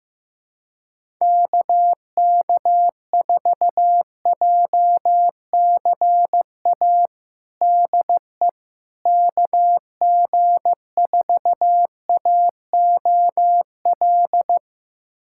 Morse code, heard as a keyed tone, about 15 wpm, 700 Hz. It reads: KK4JCA DE KG4AOL
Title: USA Callsign DE Callsign - 6 Characters - 15wpm